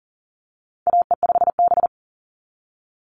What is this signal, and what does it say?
Morse code, heard as a keyed tone, about 40 wpm, 700 Hz. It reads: AE5B
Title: NA SOTA Chaser Callsigns 40wpm